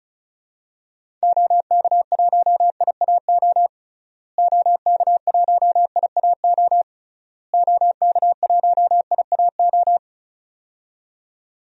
Morse code, heard as a keyed tone, about 36 wpm, 700 Hz. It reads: OK1IAO OK1IAO OK1IAO
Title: DXCC-100 Call Signs - 6 Characters - Repeated 3x - 35wpm